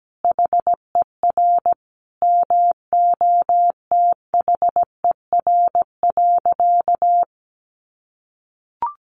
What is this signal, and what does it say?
Morse code, heard as a keyed tone, about 17 wpm, 700 Hz. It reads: HER MOTHER.
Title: Sentences from Top 400 Words 17wpm